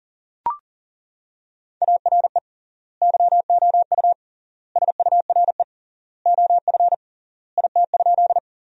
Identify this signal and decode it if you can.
Morse code, heard as a keyed tone, about 40 wpm, 700 Hz. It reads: ARE YOU SURE OF IT?
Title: 3x Word-Spacing - Sentences from Top 300 Words - 40wpm